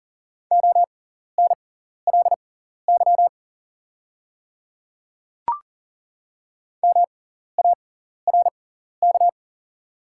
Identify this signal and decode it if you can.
Morse code, heard as a keyed tone, about 40 wpm, spaced as for 14 wpm, 700 Hz. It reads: ONLY MARK
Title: 6x Character Spacing - Top 300 Words - 40wpm